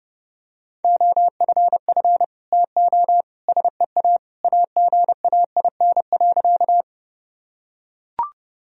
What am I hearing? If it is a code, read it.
Morse code, heard as a keyed tone, about 30 wpm, 700 Hz. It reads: OFF TO SEA AGAIN.
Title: Sentences from Top 1000 Words - Review Part 1 - 30wpm